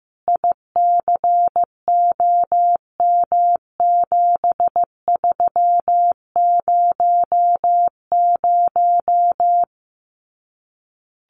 Morse code, heard as a keyed tone, about 15 wpm, 700 Hz. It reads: ICOM7300
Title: Common QSO Phrases Encore 15wpm